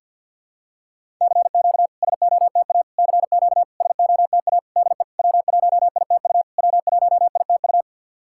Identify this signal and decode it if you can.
Morse code, heard as a keyed tone, about 52 wpm, 690 Hz. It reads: CQ SOTA CQ SOTA DE W1ITU W1ITU
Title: SOTA QSO Practice 50wpm